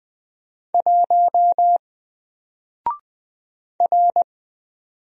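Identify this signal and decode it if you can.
Morse code, heard as a keyed tone, about 20 wpm, 700 Hz. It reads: 1 R
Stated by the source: Single letter and number Morse drill at 20 wpm, from N4LCD.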